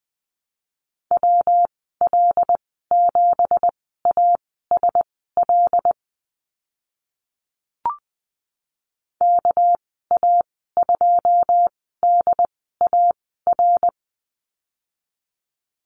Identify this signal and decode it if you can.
Morse code, heard as a keyed tone, about 20 wpm, spaced as for 15 wpm, 700 Hz. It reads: WL7ASL KA2DAR
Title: 207 Lesson 33g - Letter j - Call Signs 20wpm